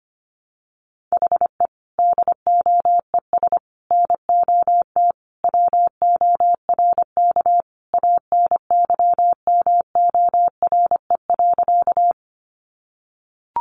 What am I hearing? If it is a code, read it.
Morse code, heard as a keyed tone, about 25 wpm, 700 Hz. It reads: HE DOES NOT WORK ANYMORE.
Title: Additional Easy Phrases 25wpm